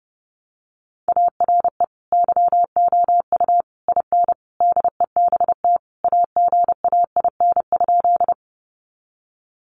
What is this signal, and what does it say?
Morse code, heard as a keyed tone, about 30 wpm, 700 Hz. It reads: ARE YOU IN DEBT AGAIN?
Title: Sentences from Top 1300 Words 30wpm